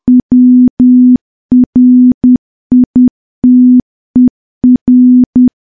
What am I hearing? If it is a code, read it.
Morse code, heard as a keyed tone, about 10 wpm, 250 Hz. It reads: WRITER
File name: morse / morse-training-10WPM